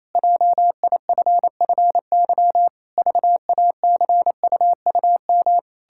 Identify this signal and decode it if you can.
Morse code, heard as a keyed tone, about 28 wpm, 700 Hz. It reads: JIFFY VACUUM
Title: Pangram Challenge v2 - 28wpm